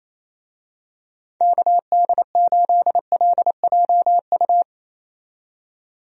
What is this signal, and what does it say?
Morse code, heard as a keyed tone, about 28 wpm, 700 Hz. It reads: KD8LJU
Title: USA Call Signs - 6 Characters - 28wpm